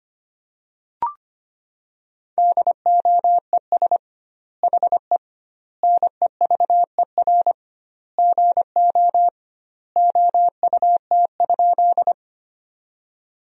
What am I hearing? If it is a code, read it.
Morse code, heard as a keyed tone, about 25 wpm, 700 Hz. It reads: DOES HE NEVER GO OUT?
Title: 2x Word-Spacing - Sentences from Top 300 Words - 25wpm